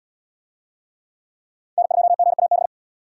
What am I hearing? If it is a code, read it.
Morse code, heard as a keyed tone, about 69 wpm, 700 Hz. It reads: N2GAB